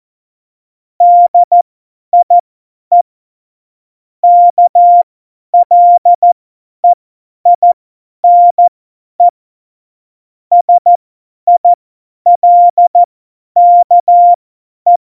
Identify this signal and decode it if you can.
Morse code, heard as a keyed tone, about 14 wpm, spaced as for 10 wpm, 700 Hz. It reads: DIE KLEINE SILKE
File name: Morsekurs / Witze / Witz 26. September 17, Tempo 50